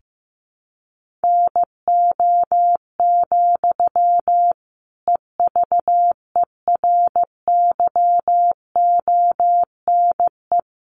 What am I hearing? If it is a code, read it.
Morse code, heard as a keyed tone, about 15 wpm, 700 Hz. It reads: NO, EVERYONE
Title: Sentences from Top 800 Words 15wpm